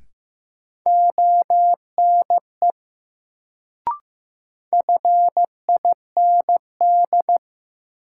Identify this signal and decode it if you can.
Morse code, heard as a keyed tone, about 15 wpm, 700 Hz. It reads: ONE FIND
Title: Sentences from Top 100 Words - New Word Drill - 15wpm